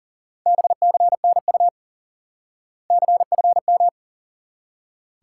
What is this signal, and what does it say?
Morse code, heard as a keyed tone, about 40 wpm, 700 Hz. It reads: BCNU CFM
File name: IQE Course 3 - Lesson 89 - YAGI - Two Elements - 40wpm